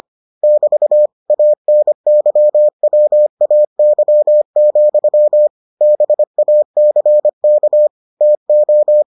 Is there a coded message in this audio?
Yes: Morse code from 0.4 to 9.0 s, heard as a keyed tone, about 25 wpm, 600 Hz. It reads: <BT> ANYWAY, BACK TO